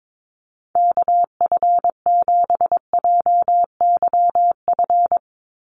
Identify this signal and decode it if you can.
Morse code, heard as a keyed tone, about 22 wpm, 700 Hz. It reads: KF7JYF
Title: USA Call Signs 22wpm